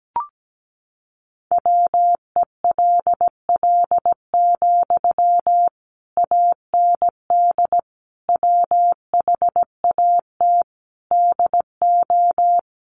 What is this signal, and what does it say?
Morse code, heard as a keyed tone, about 17 wpm, 700 Hz. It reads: WELL, AND WHAT DO
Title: Sentences from Top 200 Words 17wpm